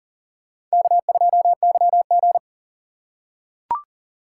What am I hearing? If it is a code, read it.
Morse code, heard as a keyed tone, about 40 wpm, 700 Hz. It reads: K2YG
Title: USA Call Signs - 4 Characters - 4x Character Spacing - 40wpm